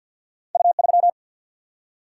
Morse code, heard as a keyed tone, about 50 wpm, 700 Hz. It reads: U3
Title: Single-Letter Single-Number 50wpm